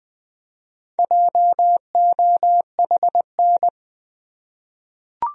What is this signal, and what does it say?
Morse code, heard as a keyed tone, about 21 wpm, 700 Hz. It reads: JOHN
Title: USA Common Names - Max 5 Letters - 20wpm